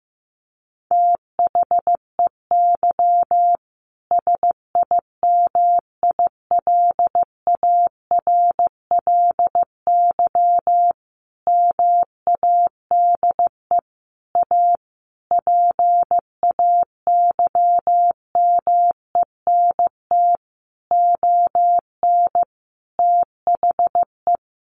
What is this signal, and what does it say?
Morse code, heard as a keyed tone, about 15 wpm, 700 Hz. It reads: THEY SIMILARLY MADE A PAYMENT ON THE
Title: Sentences from Top 1400 Words 15wpm